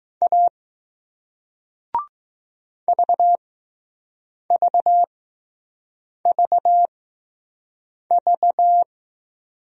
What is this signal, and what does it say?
Morse code, heard as a keyed tone, about 20 wpm, spaced as for 6 wpm, 700 Hz. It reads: A VVVV